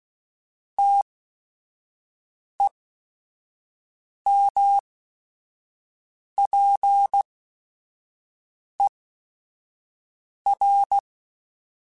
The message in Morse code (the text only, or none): TEMPER